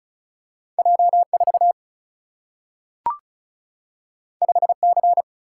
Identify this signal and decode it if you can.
Morse code, heard as a keyed tone, about 36 wpm, 700 Hz. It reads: J 4 5 C
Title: Single-Letter Single-Number 35wpm